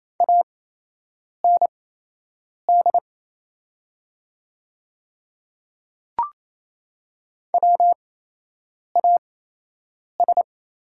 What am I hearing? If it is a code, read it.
Morse code, heard as a keyed tone, about 28 wpm, spaced as for 8 wpm, 700 Hz. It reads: AND WAS
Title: 8x Character Spacing - Top 200 Words - 28wpm